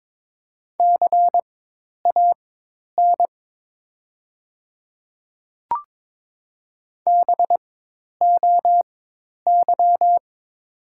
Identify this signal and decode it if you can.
Morse code, heard as a keyed tone, about 22 wpm, spaced as for 10 wpm, 700 Hz. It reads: CAN BOY
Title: 4x Character Spacing - Top 200 Words - 22wpm